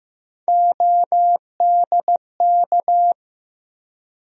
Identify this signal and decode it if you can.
Morse code, heard as a keyed tone, about 15 wpm, 700 Hz. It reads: ODK
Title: USA Call Sign Postfixes - 3 Letters 15wpm